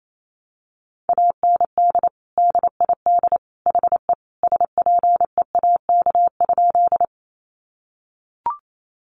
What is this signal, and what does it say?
Morse code, heard as a keyed tone, about 28 wpm, 700 Hz. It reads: AND DID HE SPEAK?
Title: Sentences from Top 1100 Words 28wpm